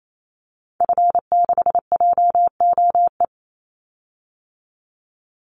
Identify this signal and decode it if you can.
Morse code, heard as a keyed tone, about 28 wpm, 700 Hz. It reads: F6JOE
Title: Call Signs Repeated 3x - 28wpm